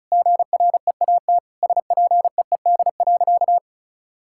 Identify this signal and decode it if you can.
Morse code, heard as a keyed tone, about 36 wpm, 700 Hz. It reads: GREAT SPEED.
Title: Sentences from Top 1000 Words - Review Part 5 - 35wpm